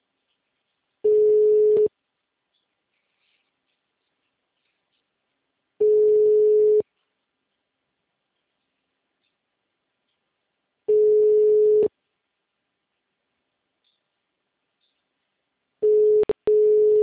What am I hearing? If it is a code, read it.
Ringback tone (Europe, 425 Hz).
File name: out- -2014- - - .0